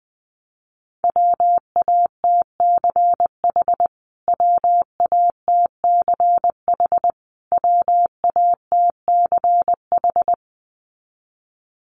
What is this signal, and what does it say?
Morse code, heard as a keyed tone, about 20 wpm, 700 Hz. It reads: WATCH WATCH WATCH
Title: Top 300 Words Repeated 20wpm